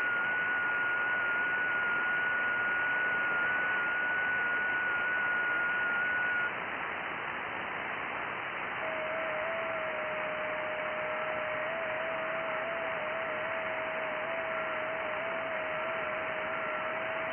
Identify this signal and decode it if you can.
FT8.
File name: IC-7100